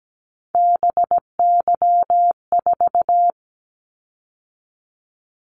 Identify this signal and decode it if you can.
Morse code, heard as a keyed tone, about 17 wpm, 700 Hz. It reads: BY4
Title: Callsign Prefixes for Top 100 DXCC - 17wpm